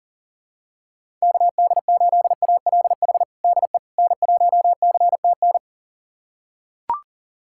Morse code, heard as a keyed tone, about 40 wpm, 700 Hz. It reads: KD8ALH DE N1CTN
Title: USA Callsign DE Callsign 40wpm